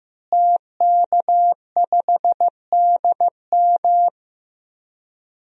Morse code, heard as a keyed tone, about 15 wpm, 700 Hz. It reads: TK5DM
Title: DXCC-100 Call Signs - 5 Characters - Repeated 2x - 15wpm